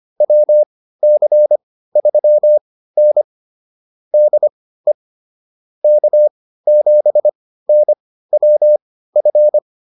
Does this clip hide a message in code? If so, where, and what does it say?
Yes: Morse code from 0.2 to 9.6 s, heard as a keyed tone, about 25 wpm, spaced as for 15 wpm, 600 Hz. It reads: WC3N DE K7NWF